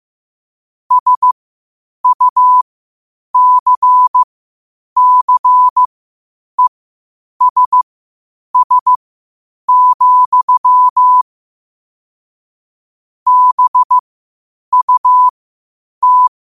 Morse code, heard as a keyed tone, about 15 wpm, spaced as for 8 wpm, 1000 Hz. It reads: SUCCESS, BUT
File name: qotdfiles / 8wpm